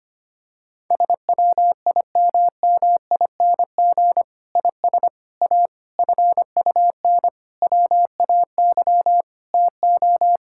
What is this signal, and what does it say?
Morse code, heard as a keyed tone, about 25 wpm, 700 Hz. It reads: SWIMMING IS A FUN WAY TO
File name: practice / 25wpm / Difficult Sentences with Hints - 25wpm